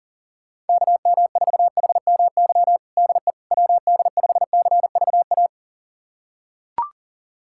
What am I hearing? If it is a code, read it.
Morse code, heard as a keyed tone, about 40 wpm, 700 Hz. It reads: KM4HMY DE WD5CVA
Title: USA Callsign DE Callsign - 6 Characters - 40wpm